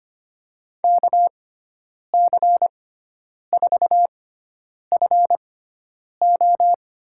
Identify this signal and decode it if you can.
Morse code, heard as a keyed tone, about 25 wpm, spaced as for 9 wpm, 700 Hz. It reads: KC4FO
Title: USA Call Signs - 5 Characters - 6x Character Spacing - 25wpm